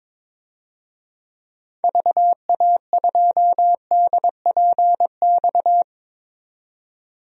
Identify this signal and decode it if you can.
Morse code, heard as a keyed tone, about 22 wpm, 700 Hz. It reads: VA2DPX
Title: DXCC-100 Call Signs - 6 Characters - 22wpm